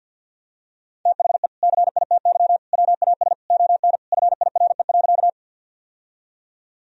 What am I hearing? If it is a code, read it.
Morse code, heard as a keyed tone, about 52 wpm, 700 Hz. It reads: THE CITY WAS ON FIRE.
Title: Sentences from Top 1000 Words - Review Part 1 - 50wpm